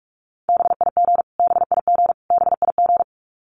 Morse code, heard as a keyed tone, about 44 wpm, 700 Hz. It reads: BIG BIG BIG
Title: Top 200 Words Repeated 45wpm